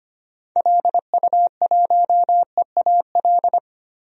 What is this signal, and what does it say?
Morse code, heard as a keyed tone, about 25 wpm, 700 Hz. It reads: LU1EAL
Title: DXCC-100 Call Signs - 6 Characters - 25wpm